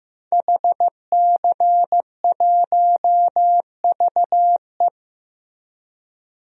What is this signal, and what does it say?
Morse code, heard as a keyed tone, about 15 wpm, 700 Hz. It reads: HC1VE
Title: DXCC-100 Call Signs - 5 Characters - Repeated 2x - 15wpm